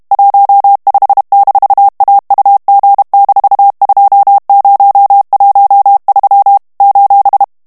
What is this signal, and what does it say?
Morse code, heard as a keyed tone, about 32 wpm, 800 Hz. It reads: 15-AUG-2013 :
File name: morse mp3 / HOLLANDAIS texte clair 2 vitesse 30 mots minute